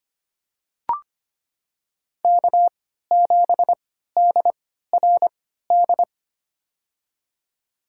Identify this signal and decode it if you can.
Morse code, heard as a keyed tone, about 25 wpm, spaced as for 14 wpm, 700 Hz. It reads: K7DRD
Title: USA Call Signs - 3x Character Spacing - 25wpm